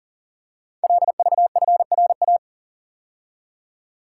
Morse code, heard as a keyed tone, about 42 wpm, 690 Hz. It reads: LVFRA
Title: Code Groups - 5 Letters - 40wpm